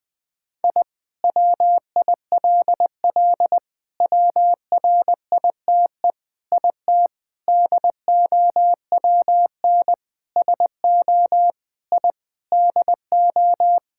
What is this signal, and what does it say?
Morse code, heard as a keyed tone, about 20 wpm, 700 Hz. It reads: I WILL WRITE IT DOWN SO I DO
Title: Sentences with 4-Word Phrases - 20wpm